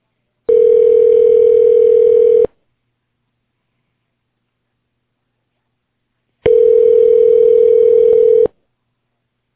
Ringback tone (North America), 440 + 480 Hz.